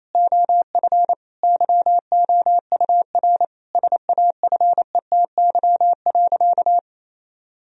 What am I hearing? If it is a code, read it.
Morse code, heard as a keyed tone, about 28 wpm, 700 Hz. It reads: OF YOUR SAFETY.
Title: Sentences from Top 1000 Words - Review Part 4 - 28wpm